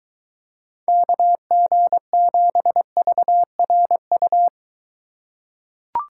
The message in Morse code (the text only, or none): KG7VRU